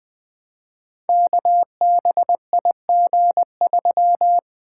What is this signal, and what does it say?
Morse code, heard as a keyed tone, about 20 wpm, 700 Hz. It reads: KBIG3
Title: Code Groups - 5 Letter:Numbers - 20wpm